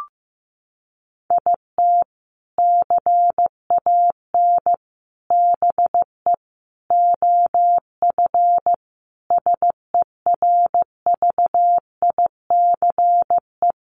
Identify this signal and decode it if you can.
Morse code, heard as a keyed tone, about 15 wpm, 700 Hz. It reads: IT CAN BE OF SERVICE
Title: Sentences from Top 1200 Words 15wpm